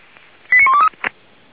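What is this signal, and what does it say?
Five-tone selective call (ZVEI), heard as quick runs of five tones, tones of about 70 ms. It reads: 89123
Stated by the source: ZVEI five-tone selective call received on an SDR.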